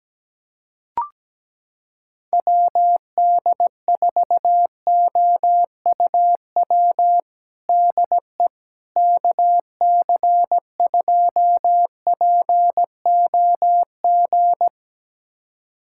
Morse code, heard as a keyed tone, about 17 wpm, 700 Hz. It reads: WD4OUW DE KC2POG
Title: USA Callsign DE Callsign 17wpm